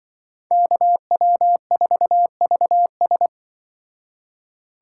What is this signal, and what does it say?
Morse code, heard as a keyed tone, about 24 wpm, 700 Hz. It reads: KW4VS
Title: Speed Racing - USA 2x2 Call Signs 20wpm